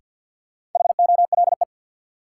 Morse code, heard as a keyed tone, about 52 wpm, 700 Hz. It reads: HOLE
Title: Sentences from Top 1600 Words - New Word Drill Alt - 50wpm